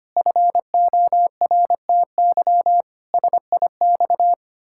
Morse code, heard as a keyed tone, about 25 wpm, 700 Hz. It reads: FORTY SIX
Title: Pangram Challenge v2 - 25wpm